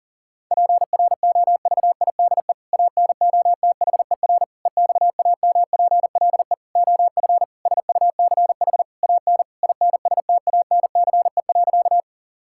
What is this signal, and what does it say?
Morse code, heard as a keyed tone, about 40 wpm, 700 Hz. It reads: PROVIDE ANOTHER EXAMPLE OF SUCH AN INSTANCE.
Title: Sentences from Top 1000 Words - Review Part 5 - 40wpm